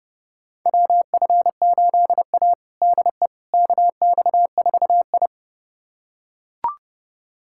Morse code, heard as a keyed tone, about 30 wpm, 700 Hz. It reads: WF8A DE KX4I
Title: USA Callsign DE Callsign - 4 Characters - 30wpm